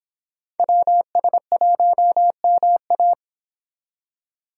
Morse code, heard as a keyed tone, about 30 wpm, 700 Hz. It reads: WS1MA